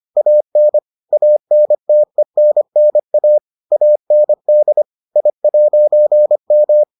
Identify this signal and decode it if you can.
Morse code, heard as a keyed tone, about 25 wpm, 600 Hz. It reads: AN ANTENNA AND I'M